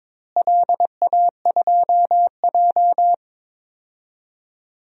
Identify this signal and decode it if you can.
Morse code, heard as a keyed tone, about 22 wpm, 700 Hz. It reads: LA2J
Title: DXCC-100 Call Signs - 4 Characters - Repeated 3x - 22wpm